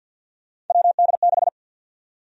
Morse code, heard as a keyed tone, about 52 wpm, 700 Hz. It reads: WD6
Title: USA Call Sign Prefixes - 50wpm